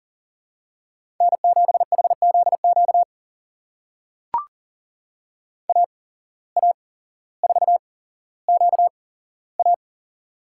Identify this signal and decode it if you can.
Morse code, heard as a keyed tone, about 37 wpm, spaced as for 17 wpm, 700 Hz. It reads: N7HZQ AA4QA
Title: USA Call Signs - 5 Characters - 8x Character Spacing - 40wpm